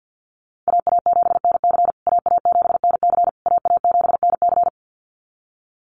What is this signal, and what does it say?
Morse code, heard as a keyed tone, about 50 wpm, 700 Hz. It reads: AA7NC AA7NC AA7NC
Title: USA Call Signs Repeated 3x 50wpm